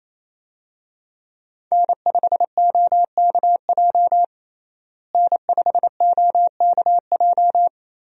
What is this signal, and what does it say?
Morse code, heard as a keyed tone, about 28 wpm, 700 Hz. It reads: N5OKJ N5OKJ
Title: USA Call Signs - 5 Characters - Repeated 2x - 28wpm